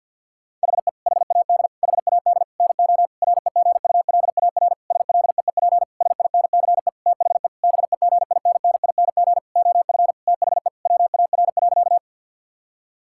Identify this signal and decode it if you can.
Morse code, heard as a keyed tone, about 51 wpm, 700 Hz. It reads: HE HAD HAD NO REGULAR SLEEP SINCE THE BEGINNING OF THE WAR.
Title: Sentences from Top 1000 Words - Review Part 6 - 50wpm